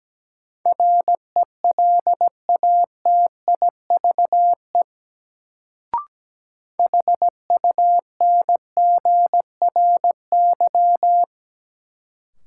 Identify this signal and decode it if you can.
Morse code, heard as a keyed tone, about 17 wpm, 700 Hz. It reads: RELATIVE HUNGRY
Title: Sentences from Top 1400 Words - New Word Drill Alt - 17wpm